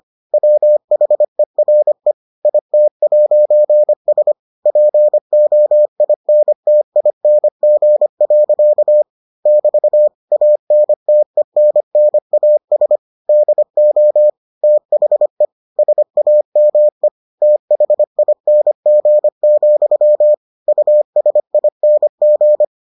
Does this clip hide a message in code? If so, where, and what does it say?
Yes: Morse code from 0.3 to 22.6 s, heard as a keyed tone, about 25 wpm, 600 Hz. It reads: WHERE IT'S POINTING. <BT> ANTENNAS DO THE SAME THING, USING